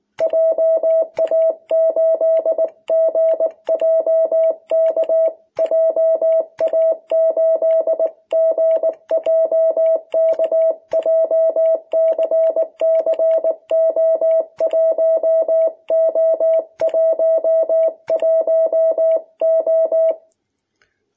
Morse code, heard as a keyed tone, about 19 wpm, 660 Hz. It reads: JA8GJK JA8GJK JCCO1O11O
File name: jh8cbh / 2021cw